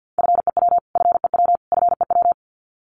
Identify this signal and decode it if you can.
Morse code, heard as a keyed tone, about 49 wpm, 700 Hz. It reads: FEW FEW FEW
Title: Top 500 Words Repeated 50wpm